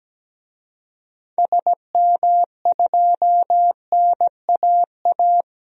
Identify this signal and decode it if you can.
Morse code, heard as a keyed tone, about 17 wpm, 700 Hz. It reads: SM2NAA
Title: DXCC-100 Call Signs - 6 Characters - Repeated 3x - 17wpm